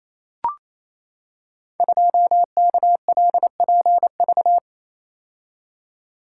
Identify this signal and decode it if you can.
Morse code, heard as a keyed tone, about 29 wpm, 700 Hz. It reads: 2KLPV
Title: Code Groups - 5 Letter:Numbers - 28wpm